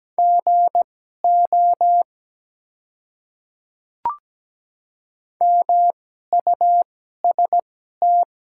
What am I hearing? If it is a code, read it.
Morse code, heard as a keyed tone, about 17 wpm, spaced as for 12 wpm, 700 Hz. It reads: GO MUST
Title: 2x Character Spacing - Top 200 Words - 17wpm